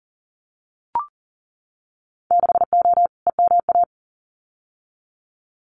Morse code, heard as a keyed tone, about 41 wpm, 700 Hz. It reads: 6O EMA
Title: Winter Field Day Exchanges Encore 40wpm